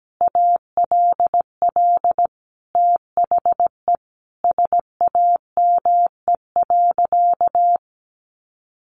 Morse code, heard as a keyed tone, about 17 wpm, 700 Hz. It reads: ALL THE SAME.
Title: Sentences from Top 1000 Words 17wpm